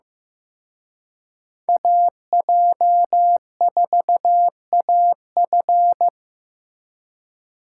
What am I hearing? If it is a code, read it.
Morse code, heard as a keyed tone, about 18 wpm, 700 Hz. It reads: AJ4AF
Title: Speed Racing - USA 2x2 Call Signs 15wpm